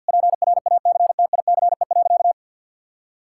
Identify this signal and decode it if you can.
Morse code, heard as a keyed tone, about 50 wpm, 700 Hz. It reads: PRACTICE.
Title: Sentences from Top 1000 Words - Review Part 5 - 50wpm